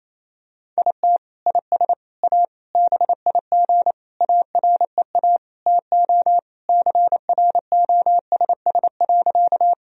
Morse code, heard as a keyed tone, about 28 wpm, 690 Hz. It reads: IT IS A BIG AREA TO CROSS.
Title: Sentences from Top 1000 Words - Review Part 2 - 28wpm